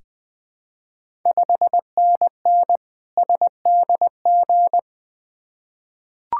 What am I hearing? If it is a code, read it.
Morse code, heard as a keyed tone, about 20 wpm, 700 Hz. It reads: 5NN SDG
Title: Ontario QSO Party Exchange 20wpm